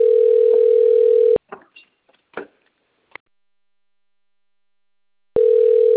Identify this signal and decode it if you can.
Ringback tone (North America), 440 + 480 Hz.
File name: rg-1600-unknown- - - .57156